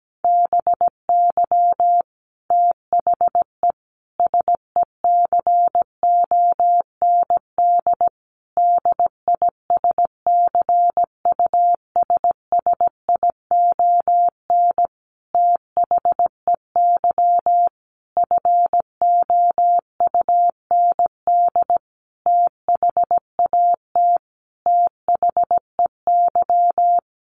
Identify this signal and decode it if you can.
Morse code, heard as a keyed tone, about 17 wpm, 700 Hz. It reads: BY THE SECOND DISCUSSION THEY FOUND THAT THEY
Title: Sentences from Top 1200 Words 17wpm